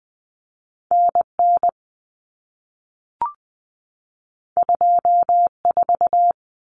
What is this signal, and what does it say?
Morse code, heard as a keyed tone, about 20 wpm, 700 Hz. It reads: NN 24
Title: Common Cut Numbers - 2-Digit 20wpm